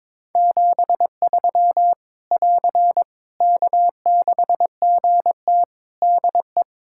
Morse code, heard as a keyed tone, about 22 wpm, 700 Hz. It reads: 73 <AR> K6GT DE
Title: Practice QSO - Exchange 4 - Ending - 22wpm